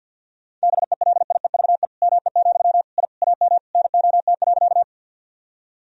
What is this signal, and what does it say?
Morse code, heard as a keyed tone, about 51 wpm, 700 Hz. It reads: BELIEVE ME, I AM NOT.
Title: Sentences from Top 1000 Words - Review Part 2 - 50wpm